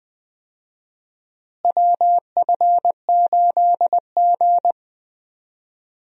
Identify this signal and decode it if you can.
Morse code, heard as a keyed tone, about 20 wpm, 700 Hz. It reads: WF8G
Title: USA Call Signs - 4 Characters - Repeated 3x - 20wpm